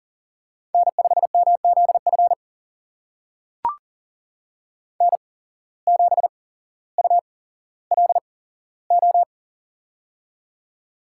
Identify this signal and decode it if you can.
Morse code, heard as a keyed tone, about 40 wpm, spaced as for 17 wpm, 700 Hz. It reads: N5MZF N7ULO
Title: USA Call Signs - 5 Characters - 8x Character Spacing - 40wpm